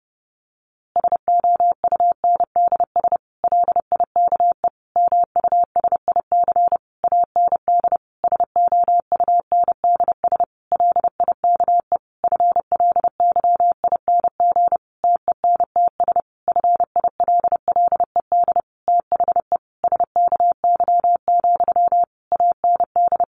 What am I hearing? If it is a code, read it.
Morse code, heard as a keyed tone, about 30 wpm, 700 Hz. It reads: SOUNDS LIKE MUSIC AND SOUNDS LIKE FLYING TENTS FILLED THE SKY, AND
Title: A Sound Of Thunder 30wpm